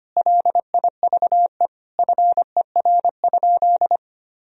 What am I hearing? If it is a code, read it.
Morse code, heard as a keyed tone, about 25 wpm, 700 Hz. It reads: LIVE FER?
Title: Sentences from Top 200 Words and Abbreviations 25wpm